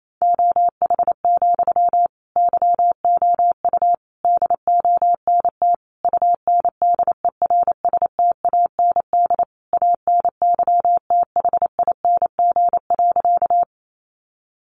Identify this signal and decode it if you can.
Morse code, heard as a keyed tone, about 28 wpm, 700 Hz. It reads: OH, YOU DONT UNDERSTAND ANYTHING.
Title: Sentences from Top 1000 Words 28wpm